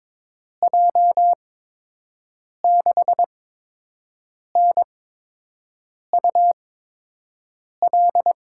Morse code, heard as a keyed tone, about 22 wpm, spaced as for 6 wpm, 700 Hz. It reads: J6NUL
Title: USA Call Signs - 6 Characters - 8x Character Spacing - 22wpm